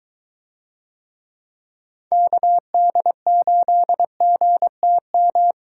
Morse code, heard as a keyed tone, about 23 wpm, 700 Hz. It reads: KD8GTM